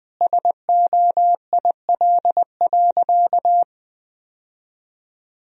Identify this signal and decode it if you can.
Morse code, heard as a keyed tone, about 20 wpm, 700 Hz. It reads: SOIL.
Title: Sentences from Top 1000 Words - Review Part 4 - 20wpm